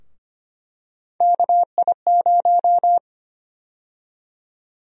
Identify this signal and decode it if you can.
Morse code, heard as a keyed tone, about 25 wpm, 700 Hz. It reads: KI0
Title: USA Call Sign Prefixes - 25wpm